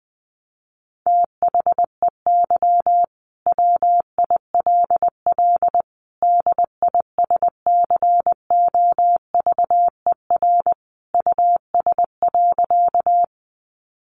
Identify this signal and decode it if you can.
Morse code, heard as a keyed tone, about 19 wpm, 700 Hz. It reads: THEY WILL DISCOVER US.
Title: Sentences from Top 1100 Words 20wpm